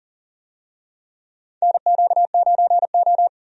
Morse code, heard as a keyed tone, about 40 wpm, 700 Hz. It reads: NQ9O
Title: USA Call Signs - 4 Characters - 40wpm